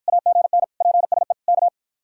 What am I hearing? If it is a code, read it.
Morse code, heard as a keyed tone, about 51 wpm, 700 Hz. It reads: AGN PSE K